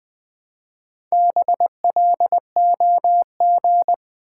Morse code, heard as a keyed tone, about 20 wpm, 700 Hz. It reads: BLOG
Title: Build A Word - 1 Letter At A Time - 4-Letter Words - 20wpm